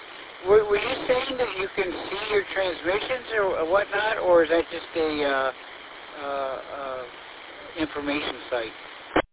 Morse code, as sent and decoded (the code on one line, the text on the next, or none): none